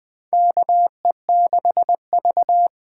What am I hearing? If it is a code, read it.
Morse code, heard as a keyed tone, about 20 wpm, 700 Hz. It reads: KE6V
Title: USA Call Signs - 6 Characters - 8x Character Spacing - 20wpm